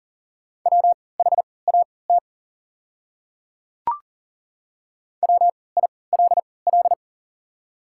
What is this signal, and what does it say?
Morse code, heard as a keyed tone, about 40 wpm, spaced as for 23 wpm, 700 Hz. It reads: WHAT WILL
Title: 3x Character Spacing - Top 100 Words - 40wpm